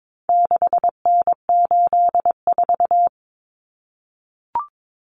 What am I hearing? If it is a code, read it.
Morse code, heard as a keyed tone, about 22 wpm, 700 Hz. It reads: 6N84
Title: Common Cut Numbers - 4-Digit 22wpm